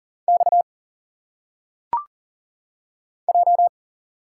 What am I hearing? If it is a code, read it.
Morse code, heard as a keyed tone, about 40 wpm, 700 Hz. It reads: X J
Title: Single Letters - Phonetic Pronunciation 40wpm